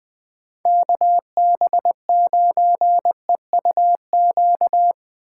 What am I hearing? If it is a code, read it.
Morse code, heard as a keyed tone, about 20 wpm, 700 Hz. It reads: KB9EUQ
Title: USA Call Signs - 2x Character Spacing - 20wpm